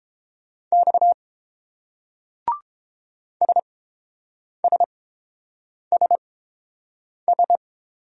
Morse code, heard as a keyed tone, about 29 wpm, spaced as for 7 wpm, 700 Hz. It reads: X SSSS